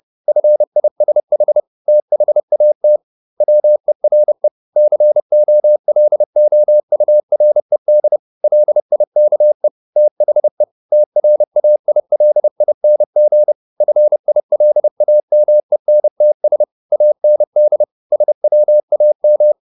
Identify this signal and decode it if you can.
Morse code, heard as a keyed tone, about 30 wpm, 600 Hz. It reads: FISH THAT WERE COLOURED LIKE THE TRAILING FILAMENTS AND SWAM
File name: the old man and the sea / the old man and the sea 1 30wpm